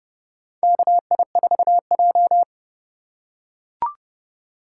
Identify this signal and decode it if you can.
Morse code, heard as a keyed tone, about 32 wpm, 700 Hz. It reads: KI4J
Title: USA Call Signs - 4 Characters - 4x Character Spacing - 30wpm